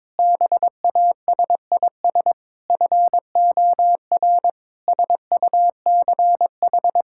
Morse code, heard as a keyed tone, about 22 wpm, 700 Hz. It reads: BASIS FOR SUCH
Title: Sentences from Top 1000 Words - Review Part 5 - 22wpm